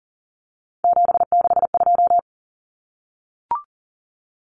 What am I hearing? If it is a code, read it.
Morse code, heard as a keyed tone, about 40 wpm, 700 Hz. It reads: 762
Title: Numbers - 3-Digit 40wpm